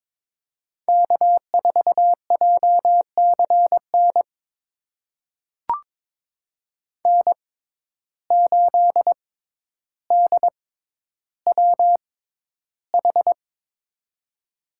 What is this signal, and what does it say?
Morse code, heard as a keyed tone, about 22 wpm, spaced as for 11 wpm, 700 Hz. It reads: K4JCN N8DWH
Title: USA Call Signs - 5 Characters - 6x Character Spacing - 22wpm